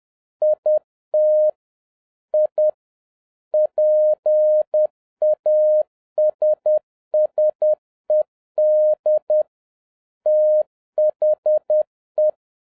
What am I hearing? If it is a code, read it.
Morse code, heard as a keyed tone, about 10 wpm, 610 Hz. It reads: IT I PASSED THE